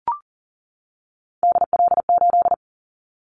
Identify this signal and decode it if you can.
Morse code, heard as a keyed tone, about 40 wpm, 700 Hz. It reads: DL8
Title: Callsign Prefixes for Top 50 DXCC - 40wpm